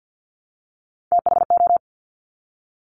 Morse code, heard as a keyed tone, about 50 wpm, 700 Hz. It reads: THO
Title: Top Three Letter Combinations in Words 50wpm